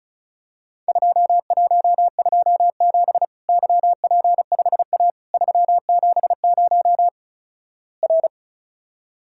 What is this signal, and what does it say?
Morse code, heard as a keyed tone, about 36 wpm, 700 Hz. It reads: 2127 YP5A 370 R
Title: Worked All Europe DX Contest - QTC Exchange - 35wpm